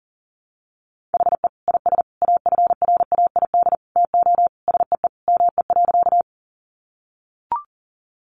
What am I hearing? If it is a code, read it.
Morse code, heard as a keyed tone, about 40 wpm, 700 Hz. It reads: HE IS AFRAID TO SEE ME.
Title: Sentences from Top 1500 Words 40wpm